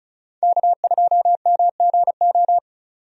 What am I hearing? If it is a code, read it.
Morse code, heard as a keyed tone, about 36 wpm, 700 Hz. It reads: K2MGO
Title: USA Call Signs - 5 Characters - 35wpm